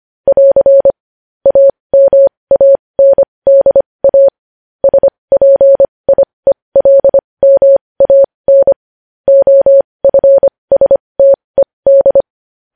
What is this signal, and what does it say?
Morse code, heard as a keyed tone, about 25 wpm, spaced as for 20 wpm, 550 Hz. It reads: <AR> AMANDA SPIELMAN OFSTED
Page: News Headlines in Morse Code at 20 WPM